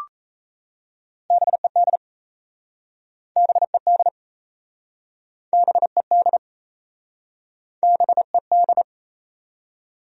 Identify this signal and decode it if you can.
Morse code, heard as a keyed tone, about 37 wpm, 700 Hz. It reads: BED BED BED BED